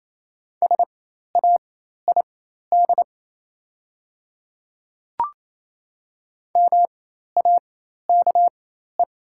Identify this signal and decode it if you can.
Morse code, heard as a keyed tone, about 28 wpm, spaced as for 13 wpm, 700 Hz. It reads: SAID MAKE
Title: 4x Character Spacing - Top 100 Words - 28wpm